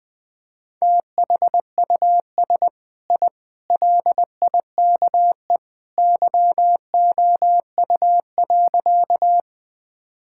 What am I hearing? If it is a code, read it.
Morse code, heard as a keyed tone, about 19 wpm, 700 Hz. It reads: THUS I LIKE YOU.
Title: Sentences from Top 1000 Words - Review Part 5 - 20wpm